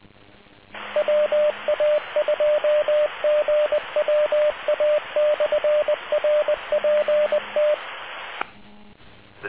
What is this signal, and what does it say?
Morse code, heard as a keyed tone, about 20 wpm, 600 Hz, with radio noise. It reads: WA2GWA/RPT